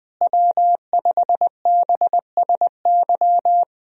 Morse code, heard as a keyed tone, about 20 wpm, 700 Hz. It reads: W5BSY
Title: USA Call Signs - 5 Characters - Repeated 2x - 20wpm